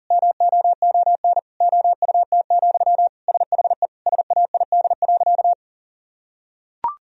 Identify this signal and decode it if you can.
Morse code, heard as a keyed tone, about 40 wpm, 700 Hz. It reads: MOON OUT, SHE SAID.
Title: Sentences from Top 1000 Words - Review Part 3 - 40wpm